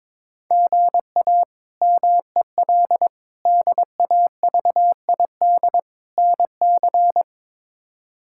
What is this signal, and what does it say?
Morse code, heard as a keyed tone, about 22 wpm, 700 Hz. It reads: GA MEL DAVID NC
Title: K1USN SST QSO Practice - 22wpm